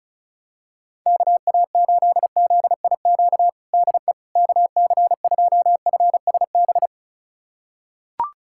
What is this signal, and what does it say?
Morse code, heard as a keyed tone, about 36 wpm, 700 Hz. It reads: KA8ZIQ DE KC2FSB
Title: USA Callsign DE Callsign 35wpm